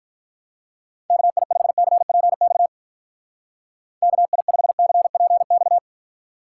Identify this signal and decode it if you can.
Morse code, heard as a keyed tone, about 44 wpm, 700 Hz. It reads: KI5CPX KI5CPX